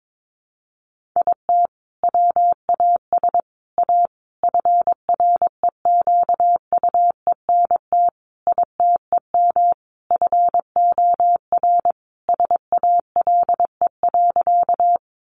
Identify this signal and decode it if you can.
Morse code, heard as a keyed tone, about 22 wpm, 700 Hz. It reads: IT WAS A FREQUENT ITEM FOR SALE.
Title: Sentences from Top 1400 Words 22wpm